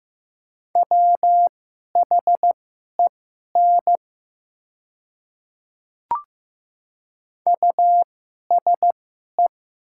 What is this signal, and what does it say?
Morse code, heard as a keyed tone, about 15 wpm, spaced as for 11 wpm, 700 Hz. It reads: WHEN USE
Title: 2x Character Spacing - Top 200 Words - 15wpm